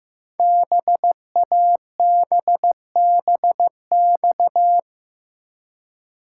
Morse code, heard as a keyed tone, about 15 wpm, 700 Hz. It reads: BABBX
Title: Code Groups - 5 Letters - 15wpm